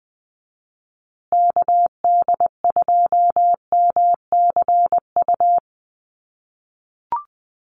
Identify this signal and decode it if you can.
Morse code, heard as a keyed tone, about 20 wpm, 700 Hz. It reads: KD2MCU
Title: USA Call Signs Repeated 2x 20wpm